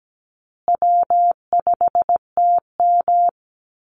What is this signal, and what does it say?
Morse code, heard as a keyed tone, about 17 wpm, 700 Hz. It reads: W5TM
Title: Easy Call Signs Repeated 3x - 17wpm